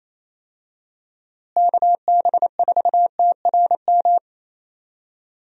Morse code, heard as a keyed tone, about 27 wpm, 700 Hz. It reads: KB4TRM
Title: USA Call Signs - 6 Characters - Repeated 2x - 28wpm